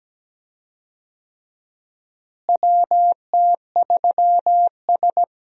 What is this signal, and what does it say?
Morse code, heard as a keyed tone, about 17 wpm, 700 Hz. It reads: WT3S
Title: USA Call Signs - 4 Characters - Repeated 3x - 17wpm